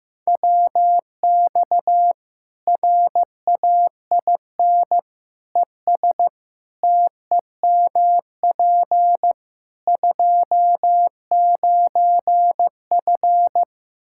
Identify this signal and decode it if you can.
Morse code, heard as a keyed tone, about 15 wpm, 700 Hz. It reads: WX RAIN ES TEMP 29F
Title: Practice QSO - Exchange 2 - Info - 15wpm